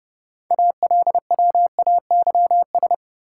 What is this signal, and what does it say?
Morse code, heard as a keyed tone, about 30 wpm, 700 Hz. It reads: ALWAYS
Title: Sentences from Top 300 Words - New Word Drill - 30wpm